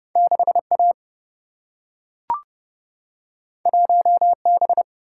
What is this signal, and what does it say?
Morse code, heard as a keyed tone, about 30 wpm, 700 Hz. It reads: 6A 1B
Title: Single-Letter Single-Number 30wpm